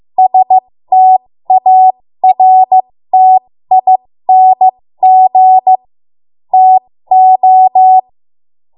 Morse code, heard as a keyed tone, about 15 wpm, spaced as for 13 wpm, 750 Hz. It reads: STARTING TO
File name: Morse / 13WPM